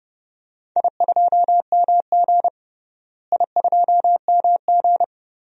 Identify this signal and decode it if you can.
Morse code, heard as a keyed tone, about 32 wpm, 700 Hz. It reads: I2MG I2MG
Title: DXCC-100 Call Signs - 4 Characters - Repeated 2x - 30wpm